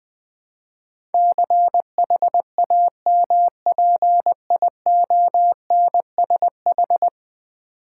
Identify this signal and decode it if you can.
Morse code, heard as a keyed tone, about 20 wpm, 700 Hz. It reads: CHAMPIONSH
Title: Build A Word - 5 Letters At A Time - 12-Letter Words - 20wpm